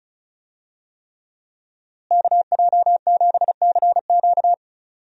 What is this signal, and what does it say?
Morse code, heard as a keyed tone, about 36 wpm, 700 Hz. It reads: KJ7CQ
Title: USA Call Signs - 5 Characters - Repeated 3x - 35wpm